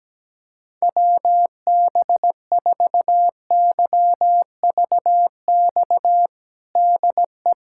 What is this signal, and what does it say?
Morse code, heard as a keyed tone, about 17 wpm, 700 Hz. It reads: WB4YVX DE
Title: USA Callsign DE Callsign - 6 Characters - 17wpm